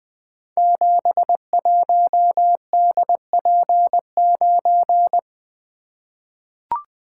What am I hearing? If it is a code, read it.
Morse code, heard as a keyed tone, about 21 wpm, 700 Hz. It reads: 71DP9
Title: Code Groups - 5 Letter:Numbers - 20wpm